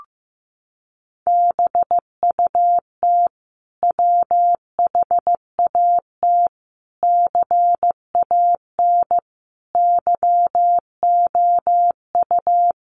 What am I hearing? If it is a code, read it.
Morse code, heard as a keyed tone, about 14 wpm, 700 Hz. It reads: BUT WHAT CAN YOU
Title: Sentences from Top 1100 Words 15wpm